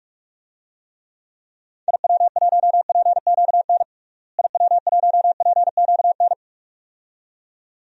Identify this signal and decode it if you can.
Morse code, heard as a keyed tone, about 46 wpm, 700 Hz. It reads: IW1PQN IW1PQN
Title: DXCC-100 Call Signs - 6 Characters - Repeated 2x - 45wpm